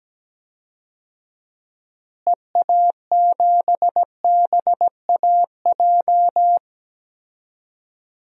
Morse code, heard as a keyed tone, about 17 wpm, 700 Hz. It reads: EA7BAJ
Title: DXCC-100 Call Signs - 6 Characters - Repeated 3x - 17wpm